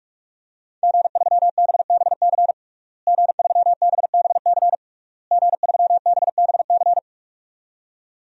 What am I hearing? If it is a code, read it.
Morse code, heard as a keyed tone, about 44 wpm, 700 Hz. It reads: G3BBC G3BBC G3BBC
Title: DXCC-100 Call Signs - 5 Characters - Repeated 3x - 45wpm